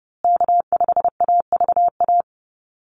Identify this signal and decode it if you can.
Morse code, heard as a keyed tone, about 30 wpm, 700 Hz. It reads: K5AVA
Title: USA Call Signs Repeated 3x 30wpm